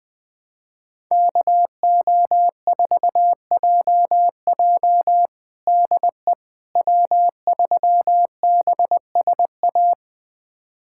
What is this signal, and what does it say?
Morse code, heard as a keyed tone, about 20 wpm, 700 Hz. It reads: KO4JJ DE W3BSA
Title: USA Callsign DE Callsign - 5 Characters - 20wpm